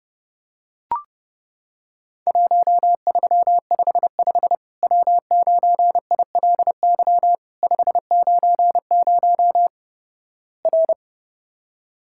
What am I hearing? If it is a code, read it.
Morse code, heard as a keyed tone, about 30 wpm, 700 Hz. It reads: 1355 W9ILY 590 R
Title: Worked All Europe DX Contest - QTC Exchange - 30wpm